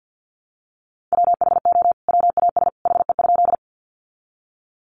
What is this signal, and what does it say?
Morse code, heard as a keyed tone, about 50 wpm, 700 Hz. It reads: WHO WAS HE?
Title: Sentences from Top 100 Words 50wpm